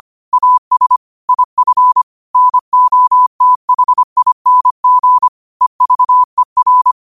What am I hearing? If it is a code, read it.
Morse code, heard as a keyed tone, about 25 wpm, 1000 Hz. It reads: AS IF NOTHING EVER
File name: qotdfiles / 25wpm